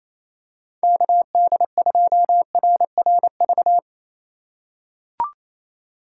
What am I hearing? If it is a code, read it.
Morse code, heard as a keyed tone, about 28 wpm, 690 Hz. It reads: KD2RRV
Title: USA Call Signs - 2x Character Spacing - 28wpm